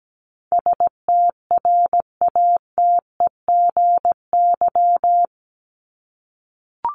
Morse code, heard as a keyed tone, about 17 wpm, 700 Hz. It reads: STRATEGY
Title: Top 501-750 Words 17wpm